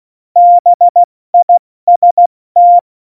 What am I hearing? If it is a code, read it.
Morse code, heard as a keyed tone, about 16 wpm, 700 Hz. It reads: BIST
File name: Morsekurs / Witze / Witz 19. Januar 2021, 70 F80